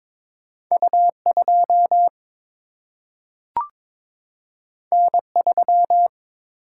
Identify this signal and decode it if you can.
Morse code, heard as a keyed tone, about 22 wpm, 700 Hz. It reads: U2 N3
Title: Single-Letter Single-Number 22wpm